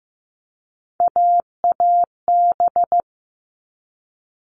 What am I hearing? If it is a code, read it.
Morse code, heard as a keyed tone, about 15 wpm, 700 Hz. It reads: AAB
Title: All Cut Numbers - 3-Digit 15wpm